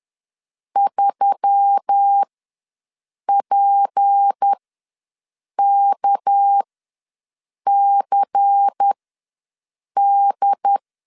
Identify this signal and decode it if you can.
Morse code, heard as a keyed tone, about 10 wpm, spaced as for 6 wpm, 800 Hz. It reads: 3PKCD
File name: LEZIONI / 6 PKX0 / 6 PK ws 10wpm 3745